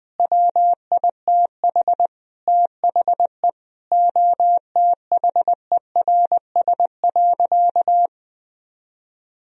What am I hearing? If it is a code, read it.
Morse code, heard as a keyed tone, about 20 wpm, 700 Hz. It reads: WITH THE OTHERS.
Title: Sentences from Top 1000 Words - Review Part 2 - 20wpm